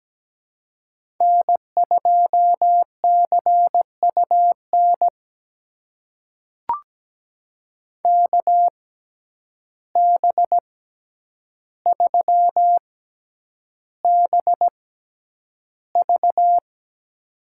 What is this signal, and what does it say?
Morse code, heard as a keyed tone, about 17 wpm, spaced as for 9 wpm, 700 Hz. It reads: N2CUN KB3BV
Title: USA Call Signs - 5 Characters - 6x Character Spacing - 17wpm